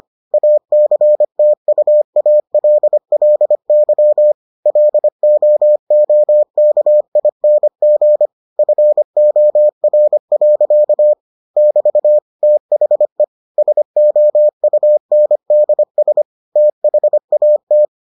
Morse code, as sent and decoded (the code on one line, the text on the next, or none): .- -.-. - ..- .- .-.. .-.. -.-- / .-.. --- --- -.- .. -. --. / ..-. --- .-. .-.-.- / -...- / - .... / ... --- ..- -. -.. ... / - .... .- -
ACTUALLY LOOKING FOR. <BT> THE SOUNDS THAT